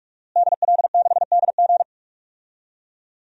Morse code, heard as a keyed tone, about 45 wpm, 700 Hz. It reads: DL6DG
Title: DXCC-100 Call Signs - 5 Characters - Repeated 3x - 45wpm